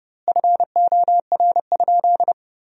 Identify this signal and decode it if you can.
Morse code, heard as a keyed tone, about 30 wpm, 700 Hz. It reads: FOR?
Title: 2x Word-Spacing - Sentences from Top 200 Words - 30wpm